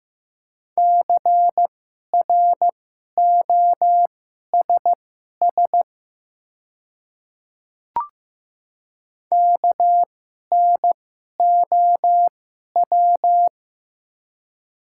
Morse code, heard as a keyed tone, about 15 wpm, spaced as for 11 wpm, 700 Hz. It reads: CROSS KNOW
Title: 2x Character Spacing - Top 300 Words - 15wpm